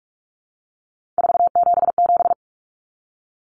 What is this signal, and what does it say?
Morse code, heard as a keyed tone, about 44 wpm, 700 Hz. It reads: 477
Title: Common Cut Numbers - 3-Digit 45wpm